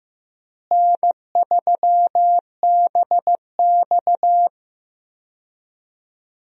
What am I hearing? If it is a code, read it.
Morse code, heard as a keyed tone, about 15 wpm, 700 Hz. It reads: N3BX
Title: USA Call Signs - 4 Characters - Repeated 2x - 15wpm